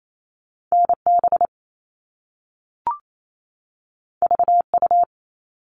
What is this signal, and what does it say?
Morse code, heard as a keyed tone, about 28 wpm, 700 Hz. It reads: NB VU
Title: All Cut Numbers - 2-Digit 28wpm